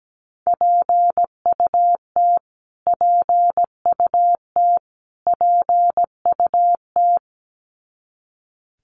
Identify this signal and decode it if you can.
Morse code, heard as a keyed tone, about 17 wpm, 700 Hz. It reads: PUT PUT PUT
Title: Top 200 Words Repeated 17wpm